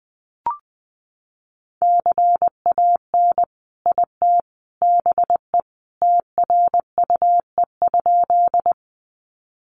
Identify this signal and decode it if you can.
Morse code, heard as a keyed tone, about 20 wpm, 700 Hz. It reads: CAN IT BE TRUE?
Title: Sentences from Top 400 Words 20wpm